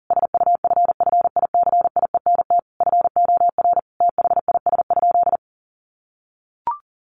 Morse code, heard as a keyed tone, about 40 wpm, 700 Hz. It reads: SUFFICIENT FOR THIS?
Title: Sentences from Top 1400 Words 40wpm